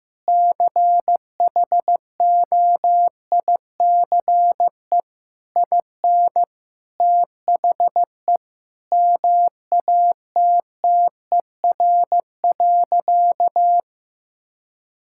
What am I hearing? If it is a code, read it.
Morse code, heard as a keyed tone, about 15 wpm, 700 Hz. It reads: CHOICE IN THE MATTER.
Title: Sentences from Top 1000 Words - Review Part 6 - 15wpm